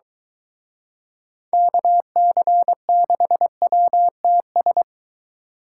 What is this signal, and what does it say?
Morse code, heard as a keyed tone, about 24 wpm, 700 Hz. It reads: KC6WTS